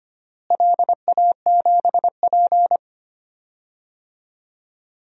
Morse code, heard as a keyed tone, about 25 wpm, 700 Hz. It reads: LA7P
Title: DXCC-100 Call Signs - 4 Characters - Repeated 2x - 25wpm